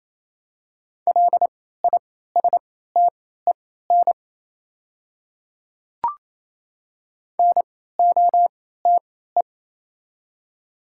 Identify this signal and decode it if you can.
Morse code, heard as a keyed tone, about 27 wpm, spaced as for 16 wpm, 700 Hz. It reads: LISTEN NOTE
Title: 3x Character Spacing - Top 500 Words - 28wpm